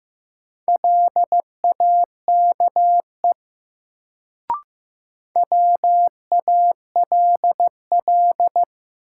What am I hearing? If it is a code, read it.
Morse code, heard as a keyed tone, about 15 wpm, 700 Hz. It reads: LAKE WALL
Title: Sentences from Top 1100 Words - New Word Drill Alt - 15wpm